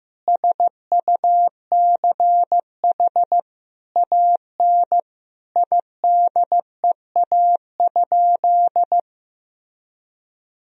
Morse code, heard as a keyed tone, about 16 wpm, 700 Hz. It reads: SUCH AN IDEA?
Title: Sentences from Top 400 Words and Abbreviations 15wpm